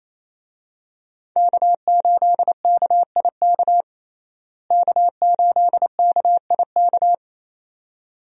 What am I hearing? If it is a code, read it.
Morse code, heard as a keyed tone, about 28 wpm, 700 Hz. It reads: K8KIK K8KIK
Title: USA Call Signs - 5 Characters - Repeated 2x - 28wpm